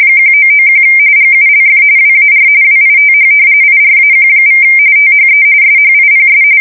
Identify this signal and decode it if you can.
RTTY, about 45.45 baud. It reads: 1991 GULF WAR WITH IRAQ= THE SHIPS
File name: fsk170 / usnavy5